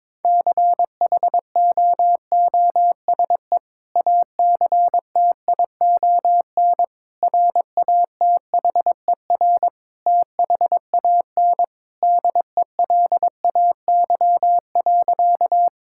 Morse code, heard as a keyed tone, about 21 wpm, 700 Hz. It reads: CHOOSE ACTION RATHER THAN DELAY.
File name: practice / 22wpm / Sentences with 2-Word Phrases - 22wpm